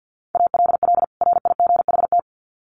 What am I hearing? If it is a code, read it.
Morse code, heard as a keyed tone, about 50 wpm, 700 Hz. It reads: ALL RIGHT
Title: Sets of 2 Words 50wpm